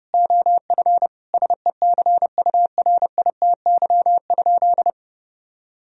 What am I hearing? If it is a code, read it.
Morse code, heard as a keyed tone, about 30 wpm, 700 Hz. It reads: OF SECURITY?
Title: Sentences from Top 1000 Words - Review Part 4 - 30wpm